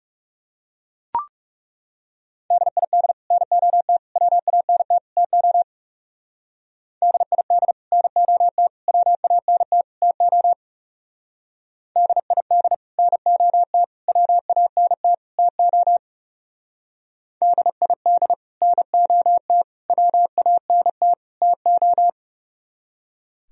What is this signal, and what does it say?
Morse code, heard as a keyed tone, about 38 wpm, 700 Hz. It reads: DID NOT WANT TO DID NOT WANT TO DID NOT WANT TO DID NOT WANT TO